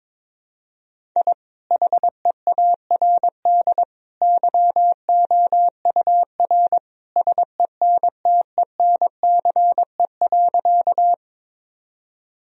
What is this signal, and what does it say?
Morse code, heard as a keyed tone, about 22 wpm, 700 Hz. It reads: I HEARD YOUR SENTENCE.
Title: Sentences from Top 1000 Words - Review Part 1 - 22wpm